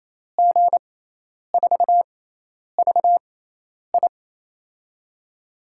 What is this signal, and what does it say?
Morse code, heard as a keyed tone, about 28 wpm, spaced as for 10 wpm, 700 Hz. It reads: G4VI
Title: USA Call Signs - 5 Characters - 6x Character Spacing - 28wpm